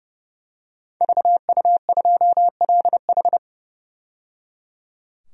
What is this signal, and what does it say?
Morse code, heard as a keyed tone, about 30 wpm, 700 Hz. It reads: VU2LH
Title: DXCC-100 Call Signs - 5 Characters - 30wpm